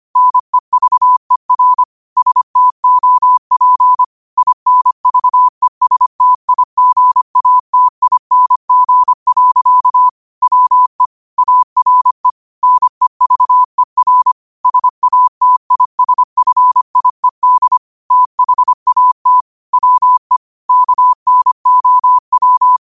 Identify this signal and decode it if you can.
Morse code, heard as a keyed tone, about 25 wpm, 1000 Hz. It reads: NEVER STOP INVESTIGATING. WE ARE NEVER SATISFIED THAT WE KNOW